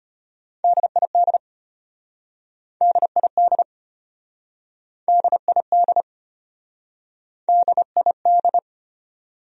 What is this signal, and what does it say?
Morse code, heard as a keyed tone, about 35 wpm, 700 Hz. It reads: DID DID DID DID